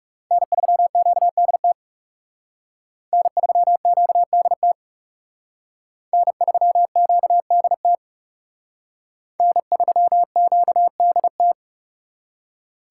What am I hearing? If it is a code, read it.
Morse code, heard as a keyed tone, about 38 wpm, 700 Hz. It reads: N3QDT N3QDT N3QDT N3QDT